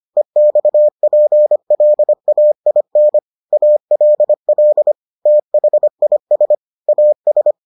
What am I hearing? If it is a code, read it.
Morse code, heard as a keyed tone, about 25 wpm, 600 Hz. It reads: EXPLAIN ALL THIS AS